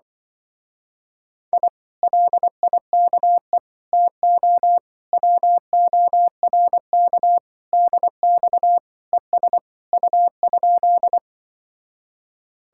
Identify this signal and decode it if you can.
Morse code, heard as a keyed tone, about 25 wpm, 700 Hz. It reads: I LIKE TO WORK DX ES U?